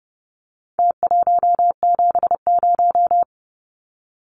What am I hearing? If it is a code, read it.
Morse code, heard as a keyed tone, about 30 wpm, 700 Hz. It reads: T170
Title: Numbers - Contest Style - 4-Digit 30wpm